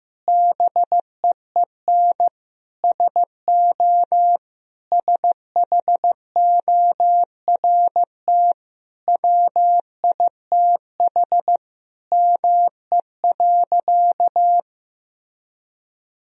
Morse code, heard as a keyed tone, about 15 wpm, 700 Hz. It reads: BEEN SO SHORT WITH ME.
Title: Sentences from Top 1000 Words - Review Part 2 - 15wpm